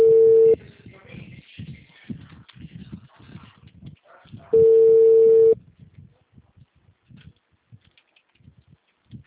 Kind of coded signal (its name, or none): telephone call progress tone